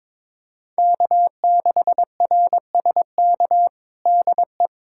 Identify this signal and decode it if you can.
Morse code, heard as a keyed tone, about 22 wpm, 700 Hz. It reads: K6RSK DE
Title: USA Callsign DE Callsign - 5 Characters - 22wpm